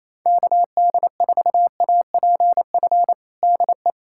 Morse code, heard as a keyed tone, about 27 wpm, 700 Hz. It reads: KD4APF DE
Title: USA Callsign DE Callsign - 6 Characters - 28wpm